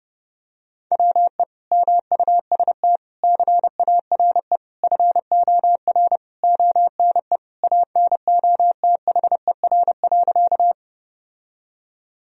Morse code, heard as a keyed tone, about 32 wpm, 700 Hz. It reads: WE MUST CARE FOR ONE ANOTHER.
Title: Sentences from Top 1000 Words - Review Part 1 - 30wpm